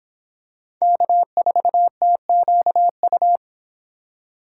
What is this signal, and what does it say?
Morse code, heard as a keyed tone, about 26 wpm, 700 Hz. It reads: K4TQU